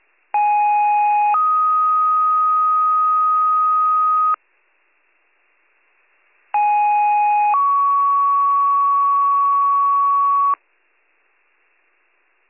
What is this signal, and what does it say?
Two-tone page, heard as a one-second tone, then a three-second tone, about 830 then 1250 Hz, about 830 then 1120 Hz. It is heard from 0.3 to 10.5 s.